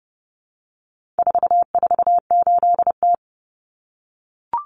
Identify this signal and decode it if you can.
Morse code, heard as a keyed tone, about 30 wpm, 700 Hz. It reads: 448T
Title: Common Cut Numbers - 4-Digit 30wpm